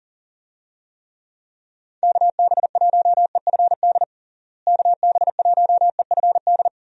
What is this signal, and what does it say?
Morse code, heard as a keyed tone, about 40 wpm, 700 Hz. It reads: KB1EFD KB1EFD
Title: USA Call Signs - 6 Characters - Repeated 2x - 40wpm